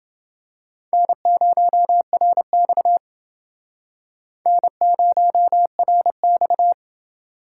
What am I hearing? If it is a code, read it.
Morse code, heard as a keyed tone, about 29 wpm, 700 Hz. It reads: N0RX N0RX